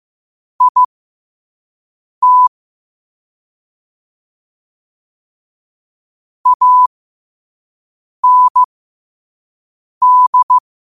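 Morse code, heard as a keyed tone, about 15 wpm, spaced as for 5 wpm, 1000 Hz. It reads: IT AND
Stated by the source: Morse code quote-of-the-day practice at 5 wpm.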